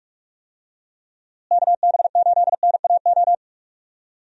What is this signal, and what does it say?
Morse code, heard as a keyed tone, about 46 wpm, 700 Hz. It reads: KB8NAO
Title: USA Call Signs - 6 Characters - Repeated 2x - 45wpm